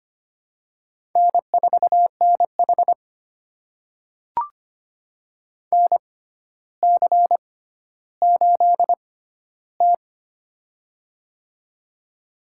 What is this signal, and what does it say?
Morse code, heard as a keyed tone, about 25 wpm, spaced as for 13 wpm, 700 Hz. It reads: N4NH NC8T
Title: USA Call Signs - 4 Characters - 6x Character Spacing - 25wpm